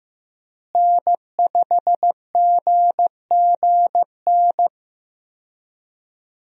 Morse code, heard as a keyed tone, about 15 wpm, 700 Hz. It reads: N5GGN
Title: USA Call Signs - 5 Characters - Repeated 3x - 15wpm